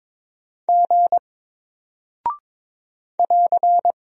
Morse code, heard as a keyed tone, about 22 wpm, 700 Hz. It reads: G <AR>
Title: Single Character (Letter:Number:Punctuation:Prosign) 22wpm